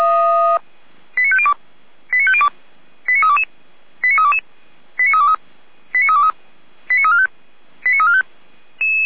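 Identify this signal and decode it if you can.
Five-tone selective call (ZVEI), heard as quick runs of five tones, tones of about 70 ms. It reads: 89502 89502 89322 89322 89323 89323 89345 89345
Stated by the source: ZVEI five-tone selective call received on an SDR, fire brigade Feuerwehr Orlamünde.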